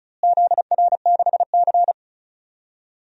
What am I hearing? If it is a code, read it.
Morse code, heard as a keyed tone, about 36 wpm, 700 Hz. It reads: ZR6C
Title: DXCC-100 Call Signs - 4 Characters - Repeated 3x - 35wpm